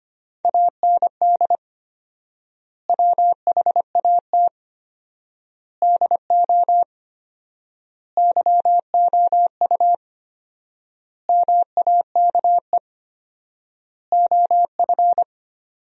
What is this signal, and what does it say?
Morse code, heard as a keyed tone, about 25 wpm, 700 Hz. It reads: AND WHAT DO YOU MAKE OF
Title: 4x Word-Spacing - Sentences from Top 100 Words - 25wpm